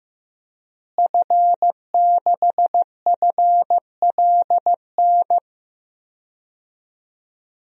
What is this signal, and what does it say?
Morse code, heard as a keyed tone, about 16 wpm, 700 Hz. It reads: F6FLN
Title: DXCC-100 Call Signs - 5 Characters - Repeated 3x - 15wpm